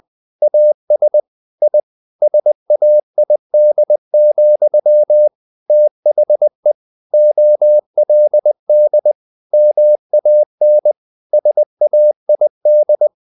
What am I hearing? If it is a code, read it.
Morse code, heard as a keyed tone, about 20 wpm, 600 Hz. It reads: AS I SAID, THE OLD MAN SAID
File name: the old man and the sea / the old man and the sea part 1 20wpm